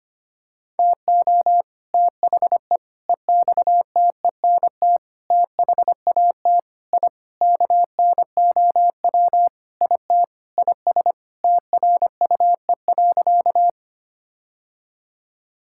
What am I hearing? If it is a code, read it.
Morse code, heard as a keyed tone, about 25 wpm, 700 Hz. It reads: TO THE EXTENT THAT I KNOW IT IS TRUE.
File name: practice / 25wpm / Sentences with 4-Word Phrases - 25wpm